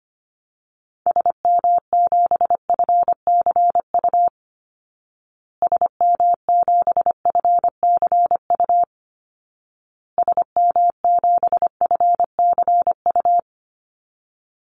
Morse code, heard as a keyed tone, about 25 wpm, 700 Hz. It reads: SM7FCU SM7FCU SM7FCU
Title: Call Signs Repeated 2x - 25wpm